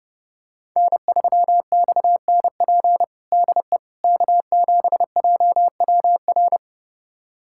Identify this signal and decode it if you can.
Morse code, heard as a keyed tone, about 32 wpm, 700 Hz. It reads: N3XNP DE K7JWR
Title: USA Callsign DE Callsign - 5 Characters - 30wpm